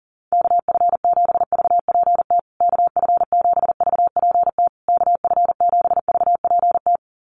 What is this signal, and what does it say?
Morse code, heard as a keyed tone, about 40 wpm, 700 Hz. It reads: KF7VPT KF7VPT KF7VPT
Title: USA Call Signs Repeated 3x 40wpm